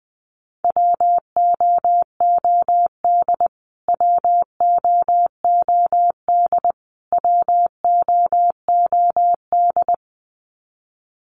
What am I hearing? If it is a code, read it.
Morse code, heard as a keyed tone, about 20 wpm, 700 Hz. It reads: WOOD WOOD WOOD
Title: Top 500 Words Repeated 20wpm